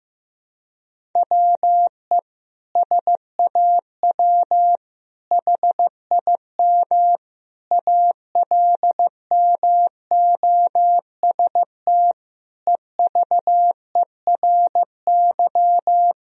Morse code, heard as a keyed tone, about 16 wpm, 700 Hz. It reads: WE SAW HIM ALMOST EVERY
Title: Sentences from Top 1000 Words - Review Part 4 - 15wpm